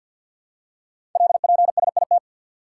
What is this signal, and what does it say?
Morse code, heard as a keyed tone, about 50 wpm, 700 Hz. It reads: LPSIT
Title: Code Groups - 5 Letters - 50wpm